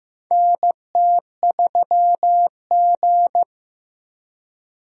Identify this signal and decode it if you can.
Morse code, heard as a keyed tone, about 15 wpm, 700 Hz. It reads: NT3G
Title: USA Call Signs - 4 Characters - Repeated 3x - 15wpm